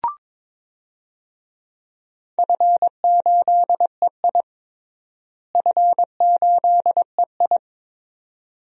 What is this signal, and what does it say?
Morse code, heard as a keyed tone, about 22 wpm, 700 Hz. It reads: F8EI F8EI
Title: DXCC-100 Call Signs - 4 Characters - Repeated 2x - 22wpm